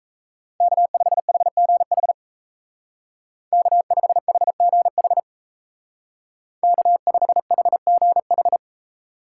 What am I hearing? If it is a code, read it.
Morse code, heard as a keyed tone, about 34 wpm, 700 Hz. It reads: K5HGH K5HGH K5HGH
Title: Speed Racing - USA Call Signs 28wpm